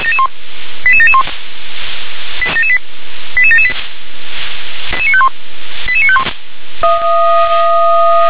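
Five-tone selective call (ZVEI), heard as quick runs of five tones, tones of about 70 ms. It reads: 88771 88778 88951
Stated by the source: ZVEI five-tone call 88923 received by SDR.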